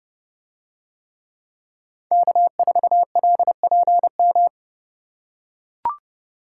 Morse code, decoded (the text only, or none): K4LPM